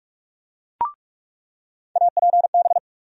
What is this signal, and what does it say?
Morse code, heard as a keyed tone, about 46 wpm, 700 Hz. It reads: APB
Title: USA Call Sign Postfixes - 3 Letters 45wpm